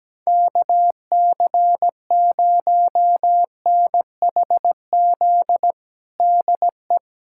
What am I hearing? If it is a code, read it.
Morse code, heard as a keyed tone, about 17 wpm, 700 Hz. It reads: KC0NHZ DE
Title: USA Callsign DE Callsign - 6 Characters - 17wpm